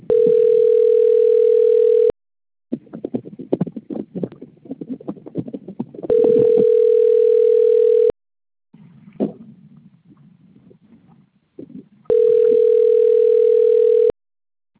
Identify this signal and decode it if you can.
Ringback tone (North America), 440 + 480 Hz.